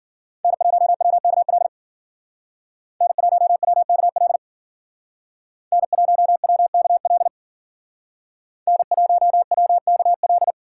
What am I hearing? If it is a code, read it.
Morse code, heard as a keyed tone, about 50 wpm, 700 Hz. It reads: N1WKL N1WKL N1WKL N1WKL
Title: Speed Racing - USA 1x3 Call Signs 40wpm